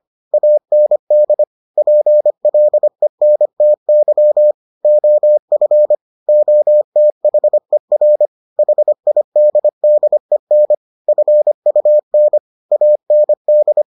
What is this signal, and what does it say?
Morse code, heard as a keyed tone, about 25 wpm, 600 Hz. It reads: AND PLENTY OF OTHER HIDDEN FUN AND